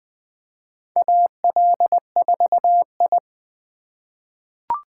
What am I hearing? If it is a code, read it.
Morse code, heard as a keyed tone, about 20 wpm, 700 Hz. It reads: AL4I
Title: USA Call Signs - 4 Characters - 3x Character Spacing - 20wpm